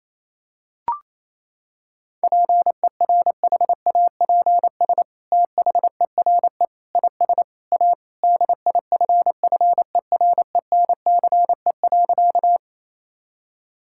Morse code, heard as a keyed tone, about 28 wpm, 700 Hz. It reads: PERHAPS THERE IS A DIFFERENCE.
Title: Sentences from Top 1000 Words - Review Part 5 - 28wpm